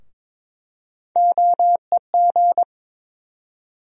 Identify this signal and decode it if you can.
Morse code, heard as a keyed tone, about 21 wpm, 700 Hz. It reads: OEG
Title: USA Call Sign Postfixes - 3 Letters 22wpm